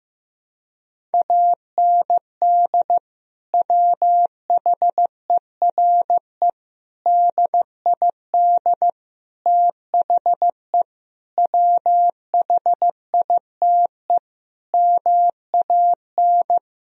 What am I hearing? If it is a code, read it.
Morse code, heard as a keyed tone, about 15 wpm, 700 Hz. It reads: AND WHERE DID THE WHITE MAN
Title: Sentences from Top 1000 Words - Review Part 1 - 15wpm